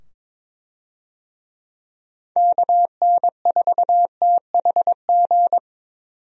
Morse code, heard as a keyed tone, about 22 wpm, 700 Hz. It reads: KN4THG
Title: USA Call Signs - 6 Characters - 22wpm